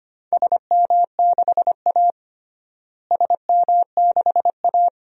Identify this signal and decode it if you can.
Morse code, heard as a keyed tone, about 25 wpm, 700 Hz. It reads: SM6A SM6A
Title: DXCC-100 Call Signs - 4 Characters - Repeated 2x - 25wpm